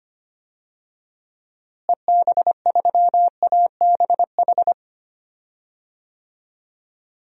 Morse code, heard as a keyed tone, about 25 wpm, 700 Hz. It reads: EB3ABH
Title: DXCC-100 Call Signs - 6 Characters - Repeated 3x - 25wpm